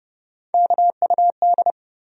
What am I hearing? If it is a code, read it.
Morse code, heard as a keyed tone, about 30 wpm, 700 Hz. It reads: KUD
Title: USA Call Sign Postfixes - 3 Letters 30wpm